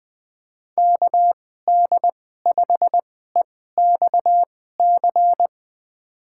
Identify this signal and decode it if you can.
Morse code, heard as a keyed tone, about 20 wpm, spaced as for 14 wpm, 700 Hz. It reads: KD5EXC
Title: USA Call Signs - 6 Characters - 2x Character Spacing - 20wpm